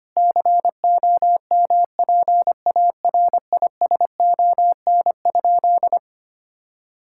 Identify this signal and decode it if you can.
Morse code, heard as a keyed tone, about 25 wpm, 700 Hz. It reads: COMPARISON?
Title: Sentences from Top 1600 Words 25wpm